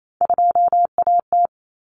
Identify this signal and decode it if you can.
Morse code, heard as a keyed tone, about 28 wpm, 700 Hz. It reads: 2AT
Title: Common Cut Numbers - 3-Digit 28wpm